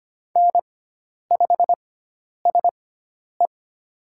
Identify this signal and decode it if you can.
Morse code, heard as a keyed tone, about 25 wpm, spaced as for 10 wpm, 700 Hz. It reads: N5SE
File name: Lesson 16g - Letter c - Call Signs 25wpm